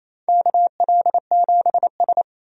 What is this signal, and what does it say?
Morse code, heard as a keyed tone, about 28 wpm, 700 Hz. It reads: KL7S
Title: DXCC-100 Call Signs - 4 Characters - 28wpm